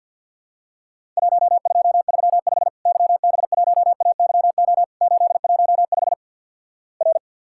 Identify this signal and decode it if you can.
Morse code, heard as a keyed tone, about 50 wpm, 700 Hz. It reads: 1235 YB1AYO 815 R
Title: Worked All Europe DX Contest - QTC Exchange - 50wpm